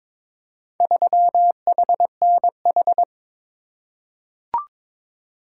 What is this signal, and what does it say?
Morse code, heard as a keyed tone, about 21 wpm, 700 Hz. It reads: 3HNH
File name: practice / 22wpm / Four Letter-Number Combinations - 22wpm